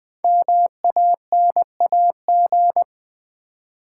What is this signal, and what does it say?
Morse code, heard as a keyed tone, about 20 wpm, 700 Hz. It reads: MANAG
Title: Build A Word - 5 Letters At A Time - 10-Letter Words - 20wpm